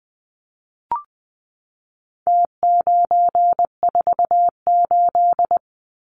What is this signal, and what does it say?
Morse code, heard as a keyed tone, about 20 wpm, 700 Hz. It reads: T948
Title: Numbers - Contest Style - 4-Digit 20wpm